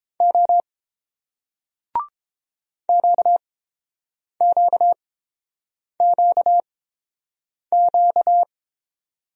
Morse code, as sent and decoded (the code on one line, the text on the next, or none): --- / --.- --.- --.- --.-
O QQQQ